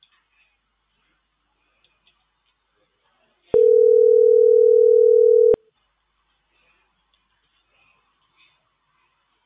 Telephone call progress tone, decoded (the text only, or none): ringback tone